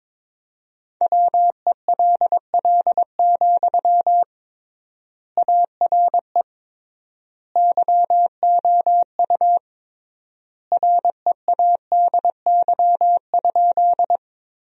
Morse code, heard as a keyed tone, about 22 wpm, 700 Hz. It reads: WELL, ARE YOU READY?
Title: 3x Word-Spacing - Sentences from Top 400 Words - 22wpm